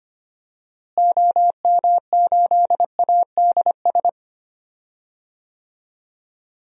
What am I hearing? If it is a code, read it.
Morse code, heard as a keyed tone, about 25 wpm, 700 Hz. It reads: OM8ADS
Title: DXCC-100 Call Signs - 6 Characters - Repeated 3x - 25wpm